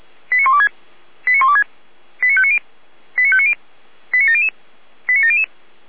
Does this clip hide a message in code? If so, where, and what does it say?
Yes: five-tone selective call (ZVEI) from 0.3 to 5.4 s, heard as quick runs of five tones, tones of about 70 ms. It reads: 89126 89126 89590 89590 89700 89700